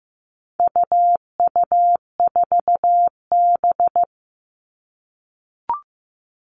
Morse code, heard as a keyed tone, about 15 wpm, 700 Hz. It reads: UU4B
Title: All Cut Numbers - 4-Digit 15wpm